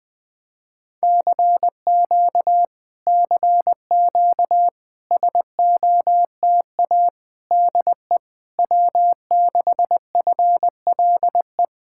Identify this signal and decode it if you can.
Morse code, heard as a keyed tone, about 19 wpm, 700 Hz. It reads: CQ CQ SOTA DE W6FLE
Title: Speed-Racing Common QSO Phrases - 20wpm